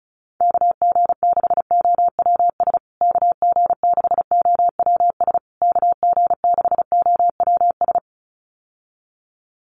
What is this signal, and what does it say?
Morse code, heard as a keyed tone, about 34 wpm, 700 Hz. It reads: KG6OWS KG6OWS KG6OWS
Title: USA Call Signs Repeated 3x 35wpm